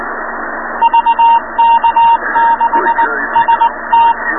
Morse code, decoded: VK6ST